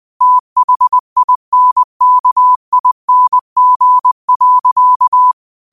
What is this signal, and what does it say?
Morse code, heard as a keyed tone, about 20 wpm, 1000 Hz. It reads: THINKING.
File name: qotdfiles / 20wpm